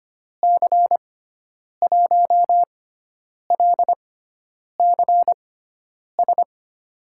Morse code, heard as a keyed tone, about 25 wpm, spaced as for 9 wpm, 700 Hz. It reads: C1LCS
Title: USA Call Signs - 6 Characters - 6x Character Spacing - 25wpm